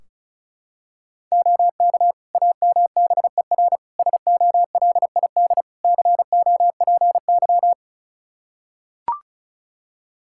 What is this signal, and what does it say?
Morse code, heard as a keyed tone, about 36 wpm, 700 Hz. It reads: OK AMBER SOLID COPY
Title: Practice QSO - Exchange 3 - More Info - 35wpm